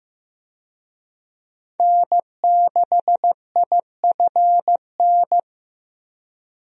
Morse code, heard as a keyed tone, about 15 wpm, 700 Hz. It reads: N6IFN
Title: USA Call Signs - 5 Characters - 15wpm